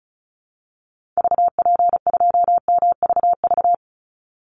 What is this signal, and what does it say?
Morse code, heard as a keyed tone, about 34 wpm, 700 Hz. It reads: VP2MVV
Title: Call Signs Repeated 2x - 35wpm